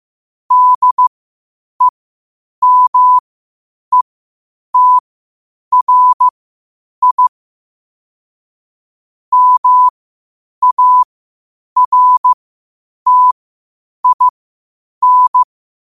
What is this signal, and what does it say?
Morse code, heard as a keyed tone, about 15 wpm, spaced as for 8 wpm, 1000 Hz. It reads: DEMETRI MARTIN
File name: qotdfiles / 8wpm